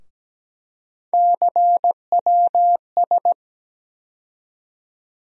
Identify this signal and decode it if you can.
Morse code, heard as a keyed tone, about 17 wpm, 700 Hz. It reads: CWS
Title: USA Call Sign Postfixes - 3 Letters 17wpm